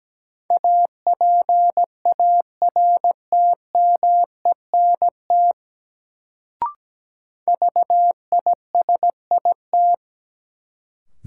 Morse code, heard as a keyed tone, about 17 wpm, 700 Hz. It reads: APARTMENT VISIT
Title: Sentences from Top 900 Words - New Word Drill - 17wpm